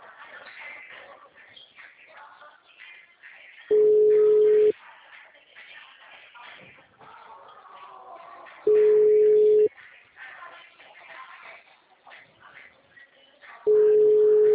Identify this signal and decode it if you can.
Ringback tone (Europe, 425 Hz).